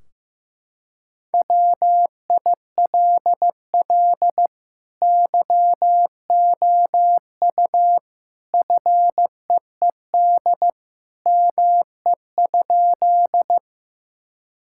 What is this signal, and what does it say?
Morse code, heard as a keyed tone, about 15 wpm, 700 Hz. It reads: WILL YOU FEED ME?
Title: Sentences from Top 1600 Words 15wpm